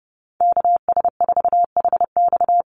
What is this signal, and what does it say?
Morse code, heard as a keyed tone, about 30 wpm, 700 Hz. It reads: KS4HX
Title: USA Call Signs 30wpm